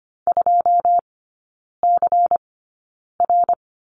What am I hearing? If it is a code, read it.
Morse code, heard as a keyed tone, about 25 wpm, spaced as for 9 wpm, 700 Hz. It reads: 2CR
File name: Lesson 17e - Letter u - Three Characters 25wpm